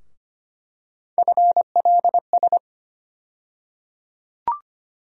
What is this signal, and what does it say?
Morse code, heard as a keyed tone, about 25 wpm, 700 Hz. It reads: FLS
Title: USA Call Sign Postfixes - 3 Letters 25wpm